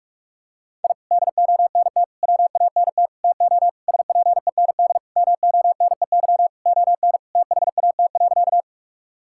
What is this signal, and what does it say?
Morse code, heard as a keyed tone, about 46 wpm, 700 Hz. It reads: I DONT WANT TO SPEND MONEY ON THAT.
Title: Sentences from Top 1000 Words - Review Part 5 - 45wpm